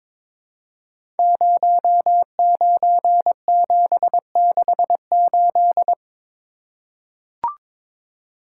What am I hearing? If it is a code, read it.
Morse code, heard as a keyed tone, about 22 wpm, 700 Hz. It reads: 09768
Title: Code Groups - 5 Letter:Numbers - 22wpm